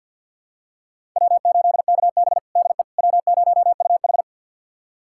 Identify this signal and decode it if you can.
Morse code, heard as a keyed tone, about 52 wpm, 700 Hz. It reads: W8KB DE W0UH
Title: USA Callsign DE Callsign - 4 Characters - 50wpm